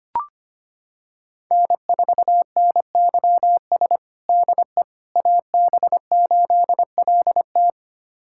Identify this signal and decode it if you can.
Morse code, heard as a keyed tone, about 25 wpm, 700 Hz. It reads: N4NYS DE AB8LT
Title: USA Callsign DE Callsign - 5 Characters - 25wpm